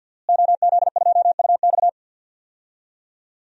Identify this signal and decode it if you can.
Morse code, heard as a keyed tone, about 52 wpm, 700 Hz. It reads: OZ2UX
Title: DXCC-100 Call Signs - 5 Characters - Repeated 3x - 50wpm